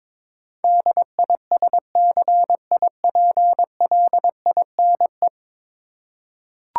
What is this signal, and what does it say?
Morse code, heard as a keyed tone, about 22 wpm, 700 Hz. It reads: DISCIPLINE
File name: practice / 22wpm / 10 Letter Words - First 3 Letters Given - 22wpm